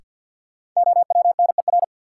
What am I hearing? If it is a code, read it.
Morse code, heard as a keyed tone, about 52 wpm, 700 Hz. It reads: OWNER
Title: Sentences from Top 1600 Words - New Word Drill - 50wpm